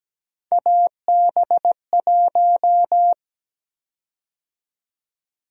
Morse code, heard as a keyed tone, about 17 wpm, 700 Hz. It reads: AB1
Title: USA Call Sign Prefixes - 17wpm